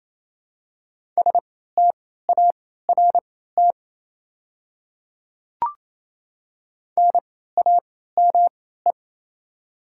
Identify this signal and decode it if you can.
Morse code, heard as a keyed tone, about 28 wpm, spaced as for 16 wpm, 700 Hz. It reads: START NAME
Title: 3x Character Spacing - Top 300 Words - 28wpm